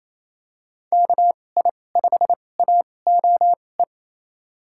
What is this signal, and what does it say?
Morse code, heard as a keyed tone, about 28 wpm, spaced as for 20 wpm, 700 Hz. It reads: KI5AOE
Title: USA Call Signs - 6 Characters - 2x Character Spacing - 28wpm